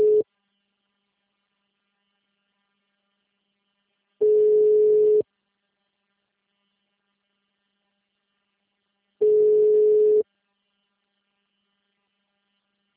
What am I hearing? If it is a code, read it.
Ringback tone (Europe, 425 Hz).